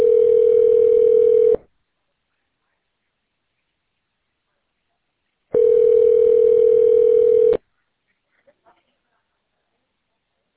Ringback tone (North America), 440 + 480 Hz.